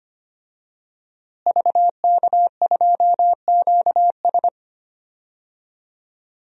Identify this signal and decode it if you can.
Morse code, heard as a keyed tone, about 25 wpm, 700 Hz. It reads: VK2QS
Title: DXCC-100 Call Signs - 5 Characters - Repeated 2x - 25wpm